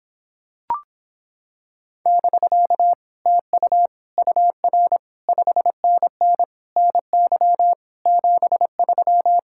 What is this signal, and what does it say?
Morse code, heard as a keyed tone, about 26 wpm, 700 Hz. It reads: <BK> TU UR 5NN NY 73